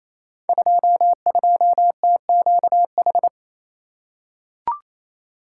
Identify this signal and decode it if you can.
Morse code, heard as a keyed tone, about 28 wpm, 700 Hz. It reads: 22TQH
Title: Code Groups - 5 Letter:Numbers - 28wpm